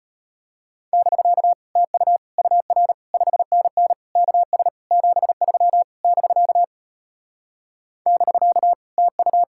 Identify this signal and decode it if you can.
Morse code, heard as a keyed tone, about 38 wpm, 700 Hz. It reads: <BK> TU UR 5NN KS 73 <BK> <BK> TU